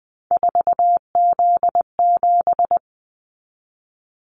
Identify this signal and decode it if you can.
Morse code, heard as a keyed tone, about 20 wpm, 700 Hz. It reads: 4Z7
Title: Callsign Prefixes for Top 50 DXCC - 20wpm